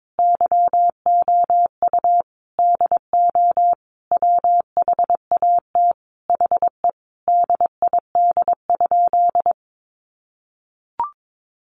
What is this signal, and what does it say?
Morse code, heard as a keyed tone, about 22 wpm, 700 Hz. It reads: YOU DO WHAT HE DID?
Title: Sentences from Top 100 Words 22wpm